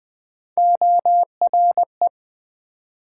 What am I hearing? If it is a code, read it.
Morse code, heard as a keyed tone, about 27 wpm, 700 Hz. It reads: ORE